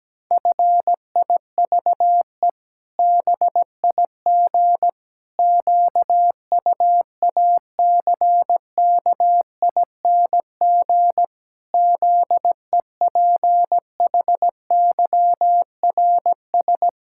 Morse code, heard as a keyed tone, about 17 wpm, 700 Hz. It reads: FIVE BIG QUACKING ZEPHYRS
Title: Pangram Challenge v2 - 17wpm